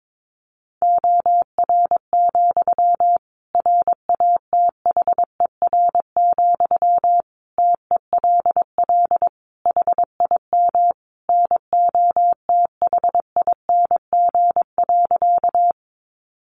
Morse code, heard as a keyed tone, about 22 wpm, 700 Hz. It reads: OR, RATHER, TELL HIM NOTHING.
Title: Sentences from Top 700 Words 22wpm